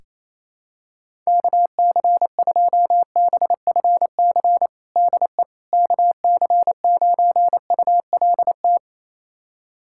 Morse code, heard as a keyed tone, about 27 wpm, 700 Hz. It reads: KC2BFC DE KC9ULT
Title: USA Callsign DE Callsign - 6 Characters - 28wpm